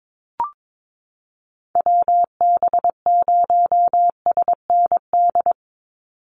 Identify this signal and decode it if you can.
Morse code, heard as a keyed tone, about 22 wpm, 700 Hz. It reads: WB0SND
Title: Call Signs Repeated 3x - 22wpm